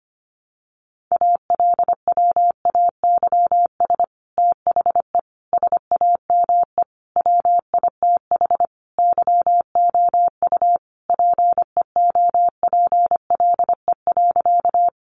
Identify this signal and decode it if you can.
Morse code, heard as a keyed tone, about 25 wpm, 700 Hz. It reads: ALWAYS THE SAME WITH YOU PEOPLE.
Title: Sentences from Top 300 Words 25wpm